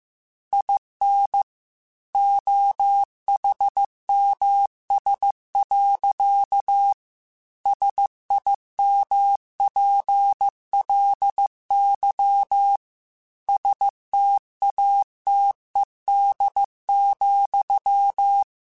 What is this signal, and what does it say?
Morse code, heard as a keyed tone, about 15 wpm, 800 Hz. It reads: IN OHMS. SIMPLY STATED,